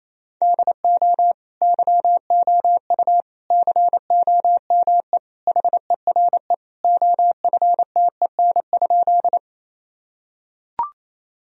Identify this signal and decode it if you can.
Morse code, heard as a keyed tone, about 28 wpm, 700 Hz. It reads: DO YOU COME HERE OFTEN?
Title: Sentences from Top 1000 Words - Review Part 1 - 28wpm